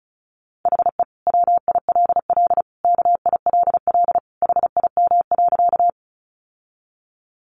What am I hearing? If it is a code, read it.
Morse code, heard as a keyed tone, about 34 wpm, 700 Hz. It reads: HE WILL KILL HIM.
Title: Sentences from Top 1400 Words 35wpm